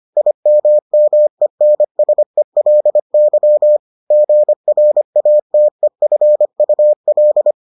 Morse code, heard as a keyed tone, about 25 wpm, 600 Hz. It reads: IMMENSELY GRATEFUL